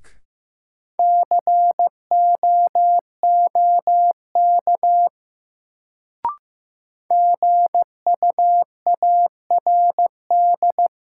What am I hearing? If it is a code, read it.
Morse code, heard as a keyed tone, about 15 wpm, 700 Hz. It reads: COOK GUARD
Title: Sentences from Top 700 Words - New Word Drill - 15wpm